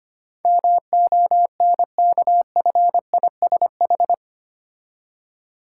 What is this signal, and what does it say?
Morse code, heard as a keyed tone, about 25 wpm, 700 Hz. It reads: MONKFISH
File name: practice / 25wpm / Difficult Words with Hints - 25wpm